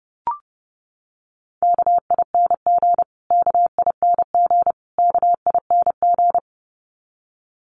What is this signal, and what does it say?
Morse code, heard as a keyed tone, about 28 wpm, 700 Hz. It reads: KING KING KING
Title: Top 500 Words Repeated 30wpm